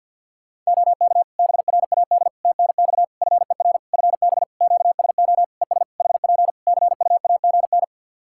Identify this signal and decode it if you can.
Morse code, heard as a keyed tone, about 50 wpm, 700 Hz. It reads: OK BRAD TNX FER FB QSO ES HP CUAGN
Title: K1USN SST QSO Practice 50wpm